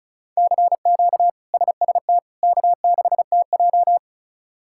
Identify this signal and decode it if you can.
Morse code, heard as a keyed tone, about 36 wpm, 700 Hz. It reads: CQ SST K6TJ
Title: K1USN SST QSO Practice - 35wpm